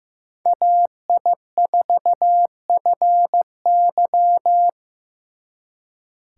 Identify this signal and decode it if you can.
Morse code, heard as a keyed tone, about 16 wpm, 700 Hz. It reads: AI4FY
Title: Speed Racing - USA 2x2 Call Signs 15wpm